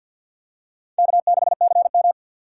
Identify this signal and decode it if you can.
Morse code, heard as a keyed tone, about 52 wpm, 700 Hz. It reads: K6CM
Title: USA Call Signs - 4x Character Spacing - 50wpm